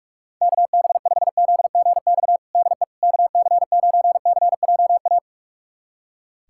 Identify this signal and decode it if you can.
Morse code, heard as a keyed tone, about 46 wpm, 700 Hz. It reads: KB5ZGX DE KC9CJA
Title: USA Callsign DE Callsign - 6 Characters - 45wpm